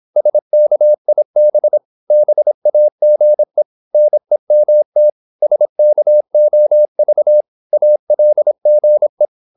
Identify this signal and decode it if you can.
Morse code, heard as a keyed tone, about 26 wpm, 600 Hz. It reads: SKIB BAGE NEMT SKOV ALGE
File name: Korte ord DK 26wpm